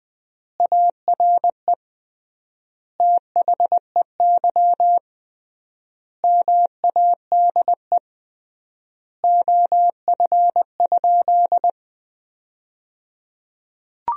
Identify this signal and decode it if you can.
Morse code, heard as a keyed tone, about 20 wpm, 700 Hz. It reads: ARE THEY MADE OF?
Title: 3x Word-Spacing - Sentences from Top 200 Words - 20wpm